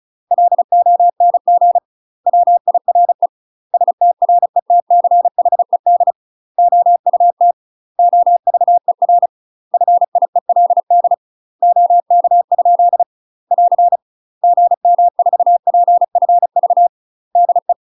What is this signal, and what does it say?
Morse code, heard as a keyed tone, about 34 wpm, 700 Hz. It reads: LONG WIRE STRETCHED OUT OVER FIELD OK? <AR> GM4PFV DE
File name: UK MASTER QSOs TRAINING / 35WPM / UK MASTER QSOs 0037